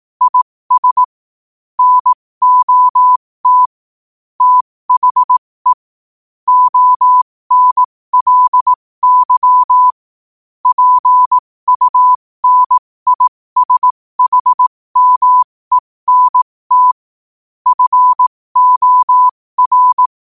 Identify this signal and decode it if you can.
Morse code, heard as a keyed tone, about 18 wpm, spaced as for 15 wpm, 1000 Hz. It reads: IS NOT THE ONLY PUNISHMENT FOR
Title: 15 WPM morse code quotes for Thu, 29 Feb 2024